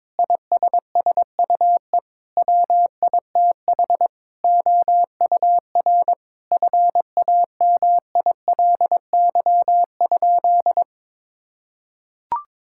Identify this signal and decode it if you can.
Morse code, heard as a keyed tone, about 22 wpm, 700 Hz. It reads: ISSUE WITH OUR FAMILY?
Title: Sentences from Top 1000 Words - Review Part 6 - 22wpm